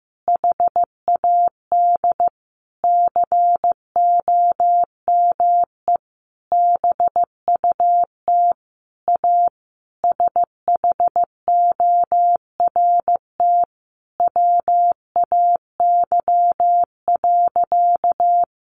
Morse code, heard as a keyed tone, about 15 wpm, 700 Hz. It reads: HAD COME BUT A SHORT WAY.
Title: Sentences from Top 1400 Words 15wpm